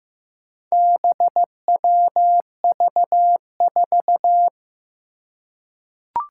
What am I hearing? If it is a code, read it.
Morse code, heard as a keyed tone, about 15 wpm, 700 Hz. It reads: BWV4
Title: Four Challenging Letter-Number Combinations 15wpm